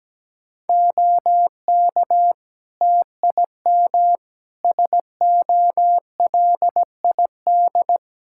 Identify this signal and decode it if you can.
Morse code, heard as a keyed tone, about 17 wpm, 700 Hz. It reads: OK TIM SOLID
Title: Practice QSO - Exchange 3 - More Info - 17wpm